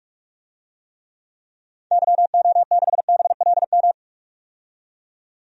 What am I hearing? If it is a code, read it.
Morse code, heard as a keyed tone, about 42 wpm, 700 Hz. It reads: YO6BLM
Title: DXCC-100 Call Signs - 6 Characters - 45wpm